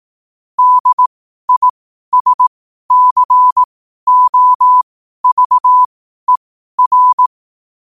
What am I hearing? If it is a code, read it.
Morse code, heard as a keyed tone, about 18 wpm, spaced as for 12 wpm, 1000 Hz. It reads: DISCOVER
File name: qotdfiles / 12wpm